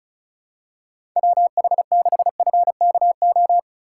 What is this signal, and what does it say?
Morse code, heard as a keyed tone, about 36 wpm, 700 Hz. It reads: WH6FKO
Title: USA Call Signs - 6 Characters - 3x Character Spacing - 35wpm